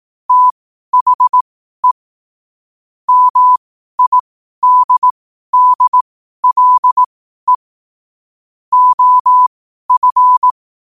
Morse code, heard as a keyed tone, about 18 wpm, spaced as for 12 wpm, 1000 Hz. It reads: THE MIDDLE OF
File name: qotdfiles / 12wpm